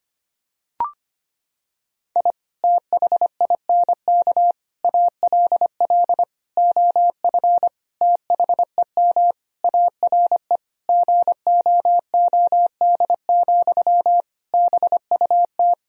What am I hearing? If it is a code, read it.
Morse code, heard as a keyed tone, about 25 wpm, 700 Hz. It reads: I THINK ALL OF THEM ARE GOOD, BUT
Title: Sentences from Top 1000 Words - Review Part 4 - 25wpm